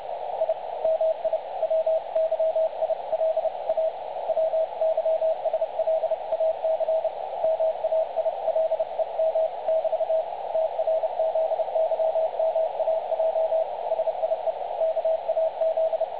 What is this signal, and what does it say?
Morse code, heard as a keyed tone, about 31 wpm, 660 Hz, with radio noise. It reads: I MI WYIL A WYSLAC MAILEM <BT> TRZYMAJ SIE KAZ
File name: dlazaawansowanych / nagraniazpasma / websdr recording 3565.0kHz